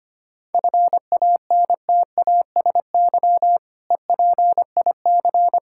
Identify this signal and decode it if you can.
Morse code, heard as a keyed tone, about 25 wpm, 700 Hz. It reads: FANTASY EPIC
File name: practice / 25wpm / Difficult Words with Hints - 25wpm